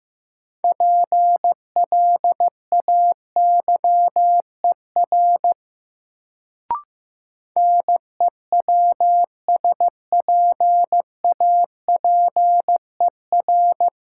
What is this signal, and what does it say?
Morse code, heard as a keyed tone, about 15 wpm, 700 Hz. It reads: PLAYER NEWSPAPER
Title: Sentences from Top 1000 Words - Review Part 4 - New Word Drill - 15wpm